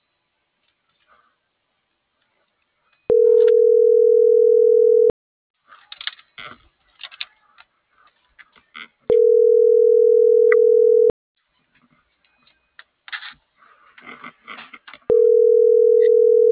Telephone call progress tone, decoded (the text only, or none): ringback tone